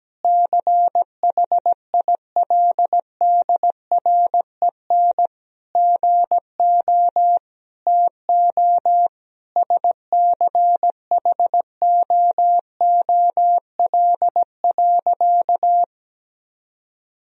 Morse code, heard as a keyed tone, about 17 wpm, 700 Hz. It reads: CHILDREN GO TO SCHOOL.
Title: Sentences from Top 1000 Words - Review Part 4 - 17wpm